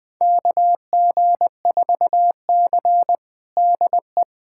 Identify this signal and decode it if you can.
Morse code, heard as a keyed tone, about 21 wpm, 700 Hz. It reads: KG4C DE
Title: USA Callsign DE Callsign - 4 Characters - 20wpm